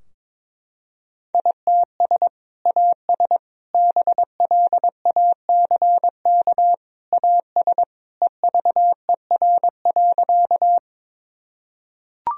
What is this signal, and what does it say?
Morse code, heard as a keyed tone, about 22 wpm, 700 Hz. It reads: ITS AS BLACK AS EVER.
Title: Sentences from Top 1000 Words - Review Part 2 - 22wpm